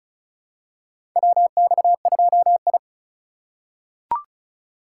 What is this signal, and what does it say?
Morse code, heard as a keyed tone, about 34 wpm, 700 Hz. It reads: WX2I
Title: USA Call Signs - 4 Characters - 2x Character Spacing - 35wpm